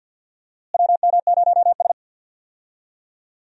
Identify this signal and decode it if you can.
Morse code, heard as a keyed tone, about 46 wpm, 700 Hz. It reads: WM0S
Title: USA Call Signs - 4 Characters - Repeated 3x - 50wpm